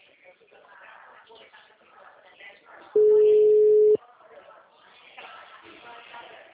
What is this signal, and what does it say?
Ringback tone (Europe, 425 Hz), with speech.